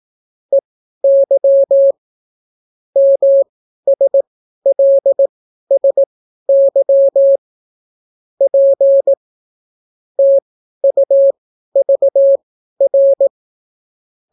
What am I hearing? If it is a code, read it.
Morse code, heard as a keyed tone, about 18 wpm, spaced as for 12 wpm, 550 Hz. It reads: EY MSLSY P TUVR